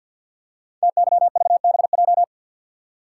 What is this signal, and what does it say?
Morse code, heard as a keyed tone, about 49 wpm, 700 Hz. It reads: TYVBJ
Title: Code Groups - 5 Letters - 50wpm